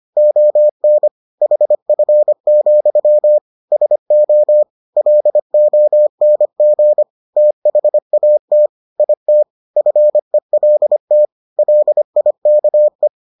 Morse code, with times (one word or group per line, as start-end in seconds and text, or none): 0.2-1.1 ON
1.4-3.4 HF,
3.7-4.6 SO
5.0-7.0 LONG
7.4-8.7 THAT
9.0-9.4 IT
9.8-11.3 FELT
11.6-13.1 LIKE